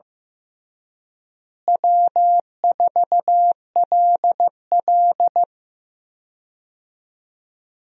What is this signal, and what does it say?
Morse code, heard as a keyed tone, about 15 wpm, 700 Hz. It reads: W4LL
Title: USA Call Signs - 4 Characters - Repeated 3x - 15wpm